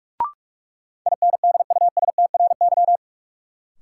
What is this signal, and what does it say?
Morse code, heard as a keyed tone, about 46 wpm, 700 Hz. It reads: INDUSTRY
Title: Sentences from Top 1000 Words - Review Part 4 - New Word Drill - 45wpm